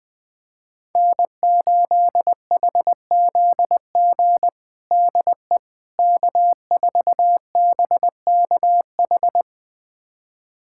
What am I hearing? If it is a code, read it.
Morse code, heard as a keyed tone, about 21 wpm, 700 Hz. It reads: N8HZG DE K4BKH
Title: USA Callsign DE Callsign 20wpm